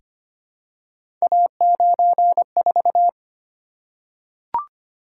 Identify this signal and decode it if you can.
Morse code, heard as a keyed tone, about 25 wpm, 700 Hz. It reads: A94
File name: Three Letter-Number Combinations - 25wpm